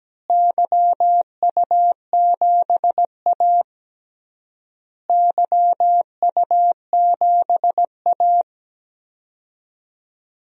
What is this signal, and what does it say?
Morse code, heard as a keyed tone, about 17 wpm, 700 Hz. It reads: YU7A YU7A
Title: DXCC-100 Call Signs - 4 Characters - Repeated 3x - 17wpm